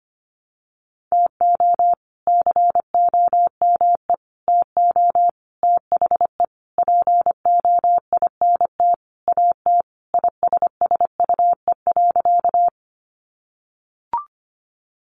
Morse code, heard as a keyed tone, about 25 wpm, 700 Hz. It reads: TO COME TO THE POINT AT ISSUE.
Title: Sentences from Top 900 Words 25wpm